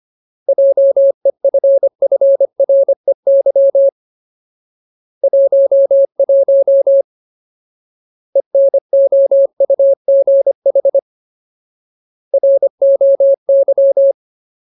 Morse code, heard as a keyed tone, about 25 wpm, 550 Hz. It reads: JEFFREY 11 ENOUGH ROY
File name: Ultimate Word List / 25WPM Ultimate